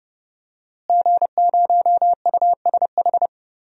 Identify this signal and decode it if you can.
Morse code, heard as a keyed tone, about 30 wpm, 700 Hz. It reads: G0USH
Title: DXCC-100 Call Signs - 5 Characters - Repeated 2x - 30wpm